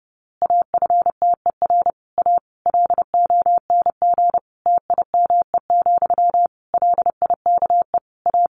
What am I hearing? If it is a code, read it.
Morse code, heard as a keyed tone, about 30 wpm, 700 Hz. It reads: AFTER A LONG TIME, LIKE A
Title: A Sound Of Thunder 30wpm